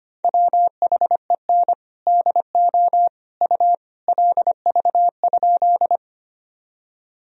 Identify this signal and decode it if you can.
Morse code, heard as a keyed tone, about 25 wpm, 700 Hz. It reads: WHEN DO U LV?
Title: Sentences from Top 400 Words and Abbreviations 25wpm